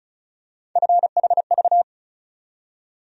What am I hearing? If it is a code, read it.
Morse code, heard as a keyed tone, about 36 wpm, 700 Hz. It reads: FHV
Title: USA Call Sign Postfixes - 3 Letters 35wpm